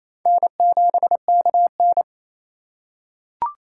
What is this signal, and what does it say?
Morse code, heard as a keyed tone, about 28 wpm, 700 Hz. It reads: N7KN
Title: USA Call Signs - 4 Characters - 4x Character Spacing - 28wpm